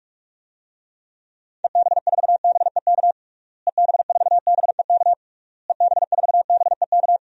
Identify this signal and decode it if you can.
Morse code, heard as a keyed tone, about 41 wpm, 700 Hz. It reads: EB4BEK EB4BEK EB4BEK
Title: DXCC-100 Call Signs - 6 Characters - Repeated 3x - 45wpm